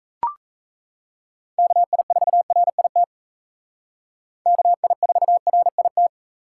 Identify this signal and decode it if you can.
Morse code, heard as a keyed tone, about 35 wpm, 700 Hz. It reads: KI4RIT KI4RIT
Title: Speed Racing - USA 2x3 Call Signs 28wpm